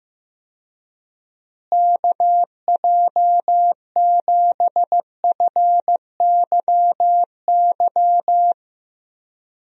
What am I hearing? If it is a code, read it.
Morse code, heard as a keyed tone, about 15 wpm, 700 Hz. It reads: KJ7FYY
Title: USA Call Signs - 6 Characters - Repeated 3x - 15wpm